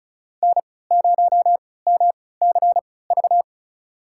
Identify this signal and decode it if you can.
Morse code, heard as a keyed tone, about 34 wpm, spaced as for 20 wpm, 700 Hz. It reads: N0MCV
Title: USA Call Signs - 5 Characters - 3x Character Spacing - 35wpm